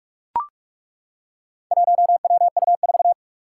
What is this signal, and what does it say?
Morse code, heard as a keyed tone, about 42 wpm, 700 Hz. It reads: 1WU4
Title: Four Challenging Letter-Number Combinations 45wpm